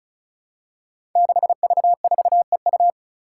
Morse code, heard as a keyed tone, about 36 wpm, 700 Hz. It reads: 6V4EU
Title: Five Challenging Letter-Number Combinations 35wpm